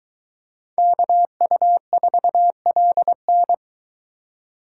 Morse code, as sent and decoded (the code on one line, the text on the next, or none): -.- ..- ....- .-.. -.
KU4LN